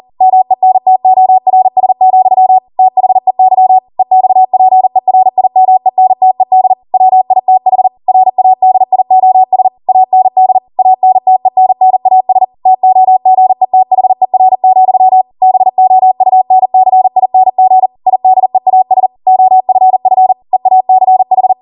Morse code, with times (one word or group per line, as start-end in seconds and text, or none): 0.2-2.6 MENTORS,
2.8-3.8 THEY
4.0-6.7 EXPERIMENTED
6.9-7.9 WITH
8.1-9.7 RADIOS
9.9-10.6 AND
10.8-12.4 ANTENNAS
12.7-15.2 TOGETHER,
15.4-17.9 BOUNCING
18.1-19.1 IDEAS
19.3-20.3 OFF
20.5-21.5 EACH